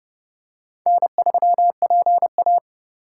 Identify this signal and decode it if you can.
Morse code, heard as a keyed tone, about 25 wpm, 700 Hz. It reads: N3PA